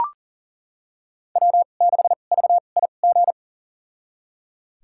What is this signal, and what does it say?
Morse code, heard as a keyed tone, about 37 wpm, spaced as for 28 wpm, 700 Hz. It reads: W6VIG
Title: USA Call Signs - 5 Characters - 2x Character Spacing - 40wpm